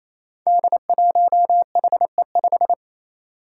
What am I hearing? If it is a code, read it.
Morse code, heard as a keyed tone, about 29 wpm, 700 Hz. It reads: D1HE5
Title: Five Challenging Letter-Number Combinations 28wpm